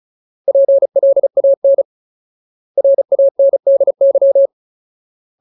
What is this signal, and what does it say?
Morse code, heard as a keyed tone, about 34 wpm, 550 Hz. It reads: PLAN RANDY
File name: Ultimate Word List / 35WPM Ultimate